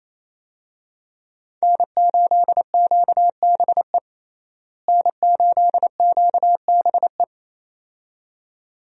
Morse code, heard as a keyed tone, about 28 wpm, 700 Hz. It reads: N8QBE N8QBE
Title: USA Call Signs - 5 Characters - Repeated 2x - 28wpm